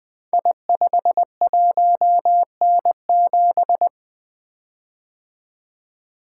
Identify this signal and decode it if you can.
Morse code, heard as a keyed tone, about 20 wpm, 700 Hz. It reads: I51N7
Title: Code Groups - 5 Letter:Numbers - 20wpm